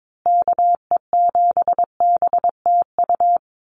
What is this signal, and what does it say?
Morse code, heard as a keyed tone, about 22 wpm, 700 Hz. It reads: KE7BTU
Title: USA Call Signs Repeated 2x 22wpm